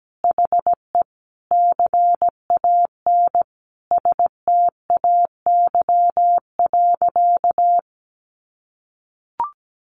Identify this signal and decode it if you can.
Morse code, heard as a keyed tone, about 17 wpm, 700 Hz. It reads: HE CAN STAY.
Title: Sentences from Top 1000 Words 17wpm